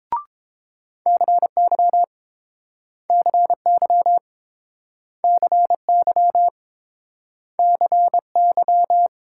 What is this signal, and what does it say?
Morse code, heard as a keyed tone, about 32 wpm, 700 Hz. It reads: CY CY CY CY